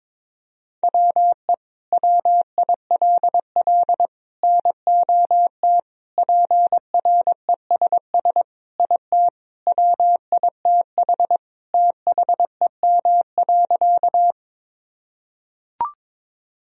Morse code, heard as a keyed tone, about 22 wpm, 700 Hz. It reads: WE WILL NOT PRESS IT WITH THEM.
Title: Sentences from Top 1000 Words - Review Part 1 - 22wpm